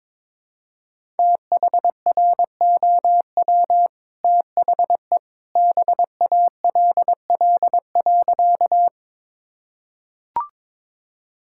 Morse code, heard as a keyed tone, about 22 wpm, 690 Hz. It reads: THROW THE BALL.
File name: practice / 22wpm / Easy Sentences with Hints - 22wpm